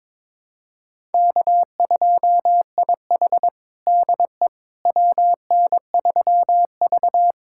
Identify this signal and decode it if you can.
Morse code, heard as a keyed tone, about 22 wpm, 700 Hz. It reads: K2IH DE WN3V
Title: USA Callsign DE Callsign - 4 Characters - 22wpm